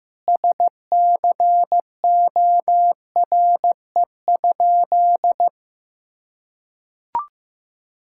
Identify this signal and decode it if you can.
Morse code, heard as a keyed tone, about 15 wpm, 700 Hz. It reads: SCORE?
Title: Sentences from Top 1600 Words 15wpm